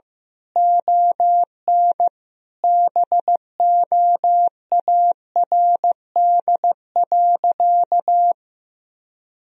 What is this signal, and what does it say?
Morse code, heard as a keyed tone, about 15 wpm, 700 Hz. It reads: ON BOARD.
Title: Sentences from Top 1000 Words - Review Part 4 - 15wpm